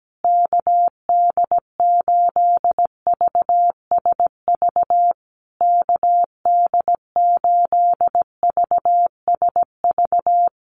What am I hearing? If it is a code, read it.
Morse code, heard as a keyed tone, about 17 wpm, 700 Hz. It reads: KD8VSV KD8VSV
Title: USA Call Signs Repeated 3x 17wpm